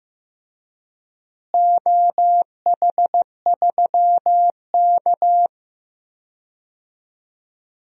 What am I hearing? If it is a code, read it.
Morse code, heard as a keyed tone, about 15 wpm, 700 Hz. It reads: OH3K
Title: DXCC-100 Call Signs - 4 Characters - Repeated 3x - 15wpm